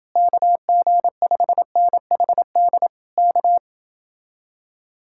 Morse code, heard as a keyed tone, about 28 wpm, 700 Hz. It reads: KG5NHD K